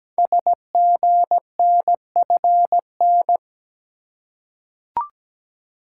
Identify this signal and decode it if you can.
Morse code, heard as a keyed tone, about 17 wpm, 700 Hz. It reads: SGNFN
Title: Code Groups - 5 Letters - 17wpm